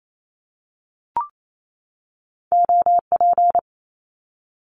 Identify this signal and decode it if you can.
Morse code, heard as a keyed tone, about 28 wpm, 700 Hz. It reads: OP
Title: Abbreviations 28wpm